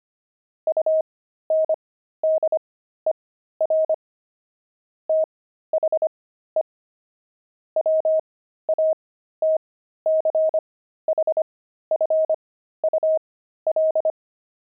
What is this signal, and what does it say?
Morse code, heard as a keyed tone, about 25 wpm, spaced as for 13 wpm, 630 Hz. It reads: UNDER THE WATCHFUL
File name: pangram-fs-13-cs-25-beehive